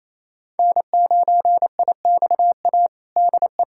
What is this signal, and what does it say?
Morse code, heard as a keyed tone, about 28 wpm, 700 Hz. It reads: N9IXA DE
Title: USA Callsign DE Callsign - 5 Characters - 28wpm